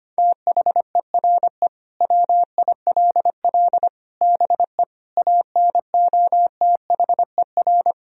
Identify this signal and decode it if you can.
Morse code, heard as a keyed tone, about 25 wpm, 700 Hz. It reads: THERE WILL BE ANOTHER
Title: Sentences from Top 1000 Words - Review Part 3 - 25wpm